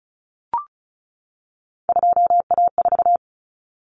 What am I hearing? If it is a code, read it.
Morse code, heard as a keyed tone, about 34 wpm, 700 Hz. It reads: 2A4
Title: Common Cut Numbers - 3-Digit 35wpm